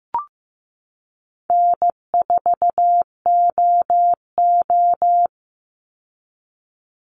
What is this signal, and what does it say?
Morse code, heard as a keyed tone, about 15 wpm, 700 Hz. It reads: N4OO
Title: Call Signs Repeated 2x - 15wpm